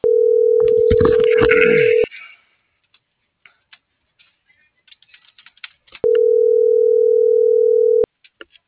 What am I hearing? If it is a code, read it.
Ringback tone (North America), 440 + 480 Hz.